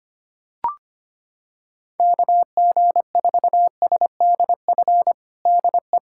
Morse code, heard as a keyed tone, about 25 wpm, 700 Hz. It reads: KG4SDF DE
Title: USA Callsign DE Callsign - 6 Characters - 25wpm